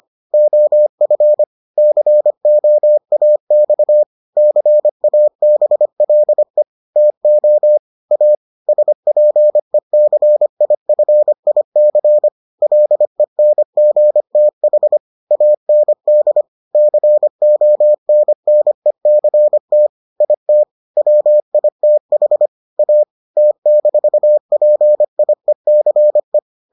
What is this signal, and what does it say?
Morse code, heard as a keyed tone, about 25 wpm, 600 Hz. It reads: OF COAX CABLE TO A SPECIFIC LENGTH AND CONNECT IT WITH A T-PIECE